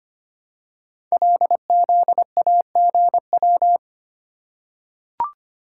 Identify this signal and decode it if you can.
Morse code, heard as a keyed tone, about 25 wpm, 700 Hz. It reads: LZAGW
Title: Code Groups - 5 Letter:Numbers - 25wpm